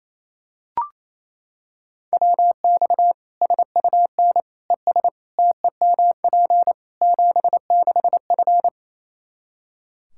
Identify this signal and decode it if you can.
Morse code, heard as a keyed tone, about 28 wpm, 700 Hz. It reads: WX SUN ES TEMP 76F
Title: K1USN SST QSO Practice 28wpm